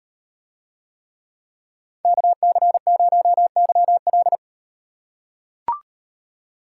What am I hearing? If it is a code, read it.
Morse code, heard as a keyed tone, about 38 wpm, 700 Hz. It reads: KC0YL